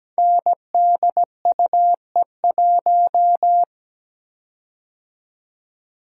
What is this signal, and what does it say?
Morse code, heard as a keyed tone, about 17 wpm, 700 Hz. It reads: NDUE1
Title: Code Groups - 5 Letter:Numbers - 17wpm